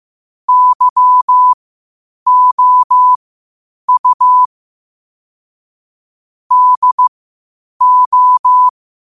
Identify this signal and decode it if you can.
Morse code, heard as a keyed tone, about 15 wpm, spaced as for 8 wpm, 1000 Hz. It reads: YOU DO
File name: qotdfiles / 8wpm